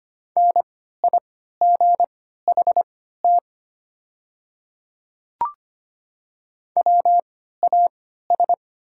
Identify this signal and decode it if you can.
Morse code, heard as a keyed tone, about 25 wpm, spaced as for 14 wpm, 700 Hz. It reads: NIGHT WAS
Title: 3x Character Spacing - Top 300 Words - 25wpm